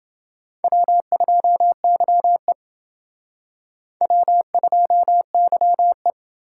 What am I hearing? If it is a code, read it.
Morse code, heard as a keyed tone, about 28 wpm, 700 Hz. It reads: W2YE W2YE